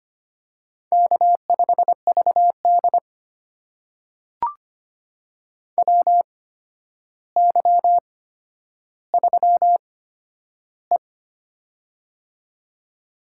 Morse code, heard as a keyed tone, about 25 wpm, spaced as for 11 wpm, 700 Hz. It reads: K5VD WY3E
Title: USA Call Signs - 4 Characters - 8x Character Spacing - 25wpm